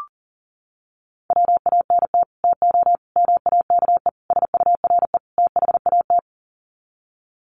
Morse code, heard as a keyed tone, about 40 wpm, 700 Hz. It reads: WANT TO MAKE SURE THAT
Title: Sets of 5 Words 40wpm